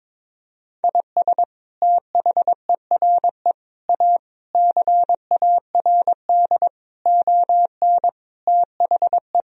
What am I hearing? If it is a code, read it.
Morse code, heard as a keyed tone, about 22 wpm, 700 Hz. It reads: IS THERE A CARD ON THE
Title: Sentences from Top 1000 Words - Review Part 4 - 22wpm